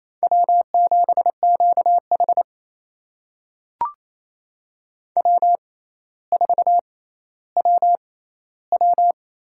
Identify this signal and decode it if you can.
Morse code, heard as a keyed tone, about 28 wpm, spaced as for 14 wpm, 700 Hz. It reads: W7QH W4WW
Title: USA Call Signs - 4 Characters - 6x Character Spacing - 28wpm